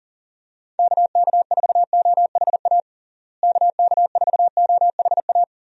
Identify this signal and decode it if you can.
Morse code, heard as a keyed tone, about 40 wpm, 700 Hz. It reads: KK4OHA KK4OHA
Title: USA Call Signs - 6 Characters - Repeated 3x - 40wpm